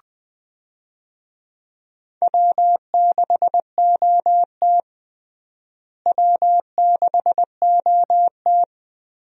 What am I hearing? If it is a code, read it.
Morse code, heard as a keyed tone, about 20 wpm, 700 Hz. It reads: W6OT W6OT
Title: USA Call Signs - 4 Characters - Repeated 3x - 20wpm